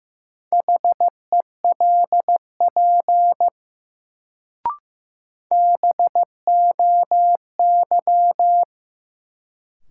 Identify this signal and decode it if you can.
Morse code, heard as a keyed tone, about 15 wpm, 700 Hz. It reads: HELP BOY
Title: Sentences from Top 200 Words - New Word Drill - 15wpm